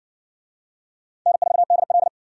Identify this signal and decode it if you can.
Morse code, heard as a keyed tone, about 60 wpm, 700 Hz. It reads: N4DL